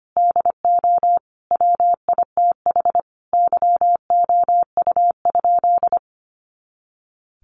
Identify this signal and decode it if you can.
Morse code, heard as a keyed tone, about 25 wpm, 700 Hz. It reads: DO WITH YOU?
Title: Sentences from Top 100 Words 25wpm